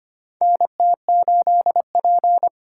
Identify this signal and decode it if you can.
Morse code, heard as a keyed tone, about 25 wpm, 700 Hz. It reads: NT8P
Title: USA Call Signs - 4 Characters - 2x Character Spacing - 25wpm